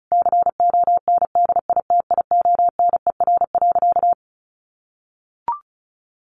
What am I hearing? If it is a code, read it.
Morse code, heard as a keyed tone, about 34 wpm, 700 Hz. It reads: CONDITIONER.
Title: Easy Phrases 35wpm